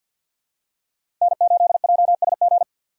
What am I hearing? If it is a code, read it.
Morse code, heard as a keyed tone, about 51 wpm, 700 Hz. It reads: N8JSG
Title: USA Call Signs - 5 Characters - 50wpm